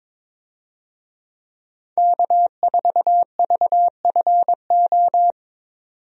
Morse code, heard as a keyed tone, about 22 wpm, 700 Hz. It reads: K4VFO
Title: USA Call Signs - 5 Characters - 22wpm